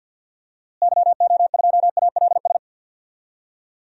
Morse code, heard as a keyed tone, about 48 wpm, 700 Hz. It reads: YO2ALS
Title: DXCC-100 Call Signs - 6 Characters - Repeated 3x - 50wpm